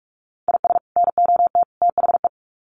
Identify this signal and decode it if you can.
Morse code, heard as a keyed tone, about 44 wpm, 700 Hz. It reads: IS NOT THE
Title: Sets of 3 Words 45wpm